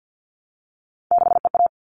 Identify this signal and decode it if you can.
Morse code, heard as a keyed tone, about 50 wpm, 700 Hz. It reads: 6EA
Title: All Cut Numbers - 3-Digit 50wpm